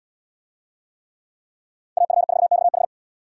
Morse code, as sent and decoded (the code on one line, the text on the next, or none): .- ..-. ....- -..- ....
AF4XH